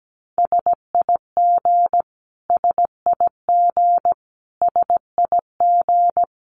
Morse code, heard as a keyed tone, about 17 wpm, 700 Hz. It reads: SIG SIG SIG
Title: Abbreviations Repeated 3x - 17wpm